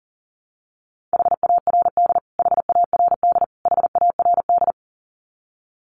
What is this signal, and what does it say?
Morse code, heard as a keyed tone, about 40 wpm, 700 Hz. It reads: HARD HARD HARD
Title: Top 500 Words Repeated 40wpm